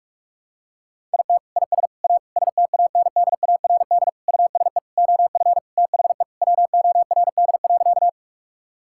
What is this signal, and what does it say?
Morse code, heard as a keyed tone, about 43 wpm, 700 Hz. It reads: IT IS A STANDARD USE OF THE WORD.
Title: Sentences from Top 1000 Words - Review Part 6 - 45wpm